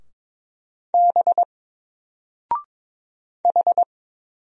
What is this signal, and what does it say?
Morse code, heard as a keyed tone, about 22 wpm, 700 Hz. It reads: B H
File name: Single-Letter-Number-22wpm